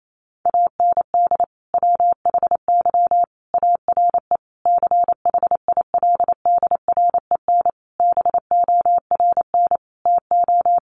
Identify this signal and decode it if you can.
Morse code, heard as a keyed tone, about 28 wpm, 700 Hz. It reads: AND WHY ARE CHILDREN BORN TO
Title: Sentences from Top 1000 Words 28wpm